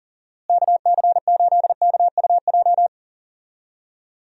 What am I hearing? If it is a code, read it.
Morse code, heard as a keyed tone, about 40 wpm, 700 Hz. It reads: KC8KUJ
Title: USA Call Signs - 6 Characters - Repeated 3x - 40wpm